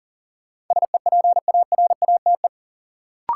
Morse code, heard as a keyed tone, about 40 wpm, 700 Hz. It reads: SEPARATE
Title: Sentences from Top 1000 Words - Review Part 7 - New Word Drill - 40wpm